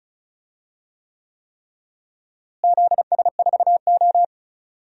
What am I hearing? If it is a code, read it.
Morse code, heard as a keyed tone, about 36 wpm, 700 Hz. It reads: ZS4O
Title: DXCC-100 Call Signs - 4 Characters - Repeated 3x - 35wpm